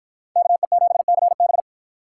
Morse code, heard as a keyed tone, about 50 wpm, 700 Hz. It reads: KE7CB